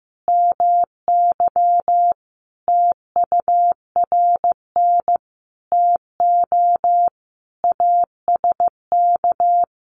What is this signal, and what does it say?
Morse code, heard as a keyed tone, about 15 wpm, 700 Hz. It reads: MY TURN TO ASK
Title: Sentences from Top 200 Words 15wpm